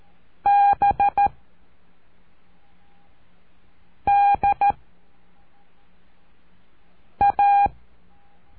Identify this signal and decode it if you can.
Morse code, heard as a keyed tone, about 13 wpm, 810 Hz. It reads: B D A
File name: cursos / CURSO DE TELEGRAFIA / Telegrafia - Trilha 04